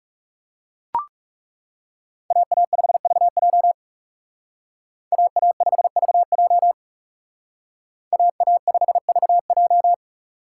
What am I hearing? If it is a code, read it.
Morse code, heard as a keyed tone, about 38 wpm, 700 Hz. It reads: AA5VJ AA5VJ AA5VJ